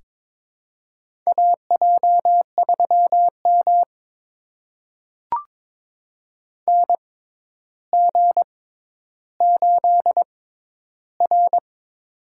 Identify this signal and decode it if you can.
Morse code, heard as a keyed tone, about 22 wpm, spaced as for 11 wpm, 700 Hz. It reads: AJ3M NG8R
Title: USA Call Signs - 6x Character Spacing - 22wpm